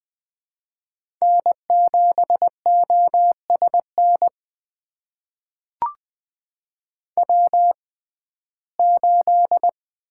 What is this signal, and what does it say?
Morse code, heard as a keyed tone, about 20 wpm, spaced as for 14 wpm, 700 Hz. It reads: N7OSN W8
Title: USA Call Signs - 5 Characters - 6x Character Spacing - 20wpm